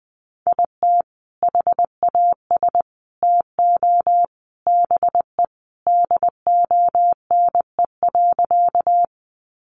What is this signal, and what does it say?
Morse code, heard as a keyed tone, about 19 wpm, 700 Hz. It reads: IT HAS TO BE DONE.
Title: Sentences from Top 500 Words 20wpm